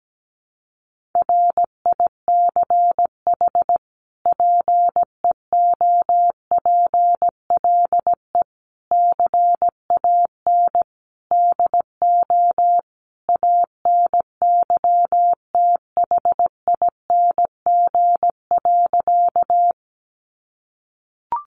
Morse code, heard as a keyed tone, about 17 wpm, 700 Hz. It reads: RICH PEOPLE CAN DO ANYTHING.
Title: Sentences from Top 1100 Words 17wpm